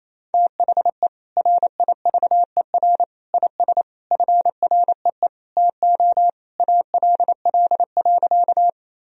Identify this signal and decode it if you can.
Morse code, heard as a keyed tone, about 28 wpm, 700 Hz. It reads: THE RIVER IS FREE TO ALL.
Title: Sentences from Top 1000 Words - Review Part 1 - 28wpm